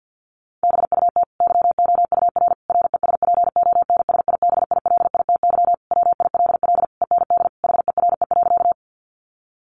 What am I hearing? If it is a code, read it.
Morse code, heard as a keyed tone, about 50 wpm, 700 Hz. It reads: BUT YOUR RESPONSIBILITY WILL END HERE.
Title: Sentences from Top 1200 Words 50wpm